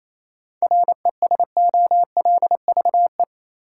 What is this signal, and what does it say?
Morse code, heard as a keyed tone, about 28 wpm, 700 Hz. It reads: RESOLVE
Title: Seven Letter Words - 28wpm